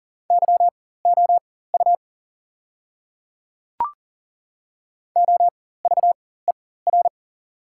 Morse code, heard as a keyed tone, about 40 wpm, spaced as for 19 wpm, 690 Hz. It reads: YOU OVER
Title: 4x Character Spacing - Top 100 Words - 40wpm